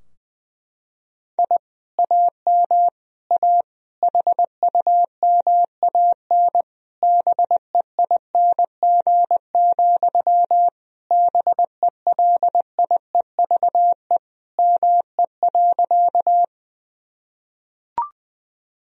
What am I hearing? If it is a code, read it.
Morse code, heard as a keyed tone, about 20 wpm, 700 Hz. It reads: I AM A HUMAN BEING, BELIEVE ME.
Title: Sentences from Top 1000 Words - Review Part 3 - 20wpm